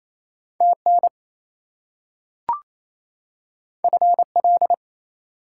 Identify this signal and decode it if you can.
Morse code, heard as a keyed tone, about 28 wpm, 700 Hz. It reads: TN FL
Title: US States and Canadian Provinces 28wpm